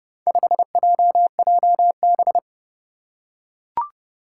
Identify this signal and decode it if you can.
Morse code, heard as a keyed tone, about 29 wpm, 700 Hz. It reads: 5JJB
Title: Four Challenging Letter-Number Combinations 30wpm